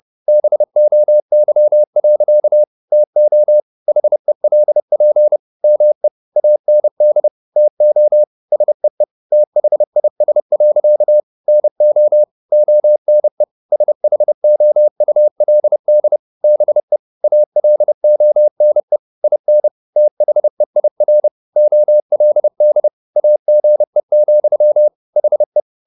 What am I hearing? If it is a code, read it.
Morse code, heard as a keyed tone, about 30 wpm, 600 Hz. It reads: BOY. TO HELP ME AND TO SEE THIS. NO ONE SHOULD BE ALONE IN THEIR OLD AGE, HE
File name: the old man and the sea / the old man and the sea 1 30wpm